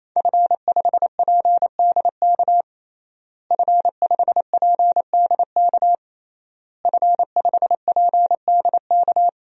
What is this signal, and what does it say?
Morse code, heard as a keyed tone, about 28 wpm, 700 Hz. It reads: F5PDK F5PDK F5PDK
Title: DXCC-100 Call Signs - 5 Characters - Repeated 3x - 28wpm